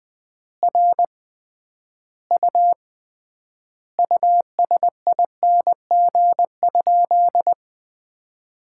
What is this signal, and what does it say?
Morse code, heard as a keyed tone, about 20 wpm, 700 Hz. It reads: R U USING?
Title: Common QSO Phrases - 3x Word-Spacing - 20wpm